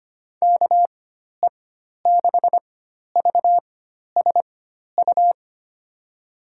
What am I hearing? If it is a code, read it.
Morse code, heard as a keyed tone, about 25 wpm, spaced as for 12 wpm, 700 Hz. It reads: KE6VSU
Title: USA Call Signs - 6 Characters - 4x Character Spacing - 25wpm